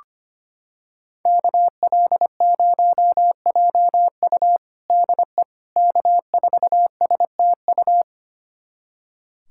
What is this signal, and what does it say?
Morse code, heard as a keyed tone, about 25 wpm, 700 Hz. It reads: KL0JU DE K4STU
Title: USA Callsign DE Callsign - 5 Characters - 25wpm